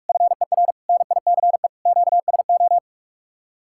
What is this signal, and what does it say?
Morse code, heard as a keyed tone, about 50 wpm, 700 Hz. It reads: FER NICE QSO